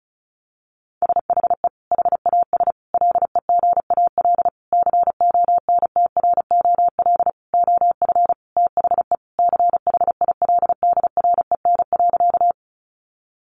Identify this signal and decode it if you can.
Morse code, heard as a keyed tone, about 34 wpm, 700 Hz. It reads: SHE HAS LEGAL CONTROL OF THE CHILDREN.
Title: Sentences from Top 1200 Words 35wpm